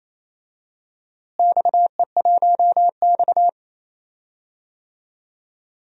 Morse code, heard as a keyed tone, about 28 wpm, 700 Hz. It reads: XE1X
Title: DXCC-100 Call Signs - 4 Characters - Repeated 3x - 28wpm